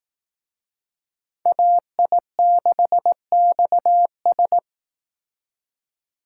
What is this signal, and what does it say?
Morse code, heard as a keyed tone, about 18 wpm, 700 Hz. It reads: AI6XS
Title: Speed Racing - USA 2x2 Call Signs 15wpm